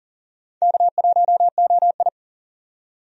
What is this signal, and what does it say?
Morse code, heard as a keyed tone, about 40 wpm, 700 Hz. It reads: K1OI
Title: USA Call Signs - 4 Characters - 40wpm